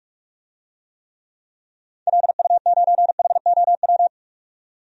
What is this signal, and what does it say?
Morse code, heard as a keyed tone, about 43 wpm, 700 Hz. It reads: LU9HOW
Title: DXCC-100 Call Signs - 6 Characters - 45wpm